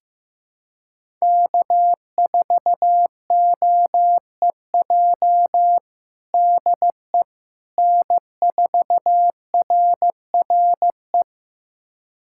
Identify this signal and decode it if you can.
Morse code, heard as a keyed tone, about 15 wpm, 700 Hz. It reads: K4OEJ DE N4RRE
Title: USA Callsign DE Callsign - 5 Characters - 15wpm